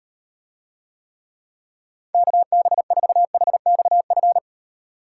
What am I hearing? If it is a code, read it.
Morse code, heard as a keyed tone, about 38 wpm, 700 Hz. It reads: KB4HXF